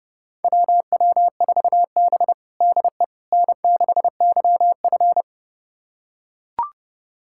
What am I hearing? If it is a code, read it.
Morse code, heard as a keyed tone, about 30 wpm, 700 Hz. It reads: WW4B DE N6YF
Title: USA Callsign DE Callsign - 4 Characters - 30wpm